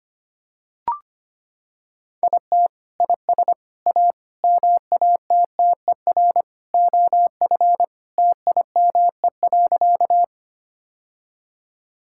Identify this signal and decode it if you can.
Morse code, heard as a keyed tone, about 25 wpm, 700 Hz. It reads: IT IS A MATTER OF TIME.
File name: practice / 25wpm / Sentences with 3-Word Phrases - 25wpm